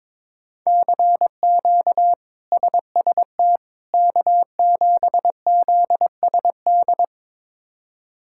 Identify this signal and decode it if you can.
Morse code, heard as a keyed tone, about 21 wpm, 700 Hz. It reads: CQ SST K7ZSD
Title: K1USN SST QSO Practice - 22wpm